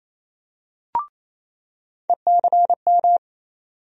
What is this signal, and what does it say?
Morse code, heard as a keyed tone, about 28 wpm, 700 Hz. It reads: ECM
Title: USA Call Sign Postfixes - 3 Letters 28wpm